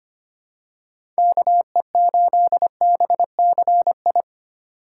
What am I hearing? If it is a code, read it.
Morse code, heard as a keyed tone, about 25 wpm, 700 Hz. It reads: KE8BCI
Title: USA Call Signs - 6 Characters - 3x Character Spacing - 25wpm